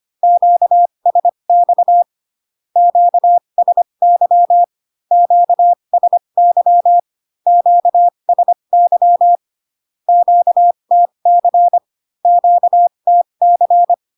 Morse code, heard as a keyed tone, about 25 wpm, spaced as for 22 wpm, 700 Hz. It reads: QSX QSY QSY QSY QTC QTC
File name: Q-codes most common repeated 3 times 18 25wpm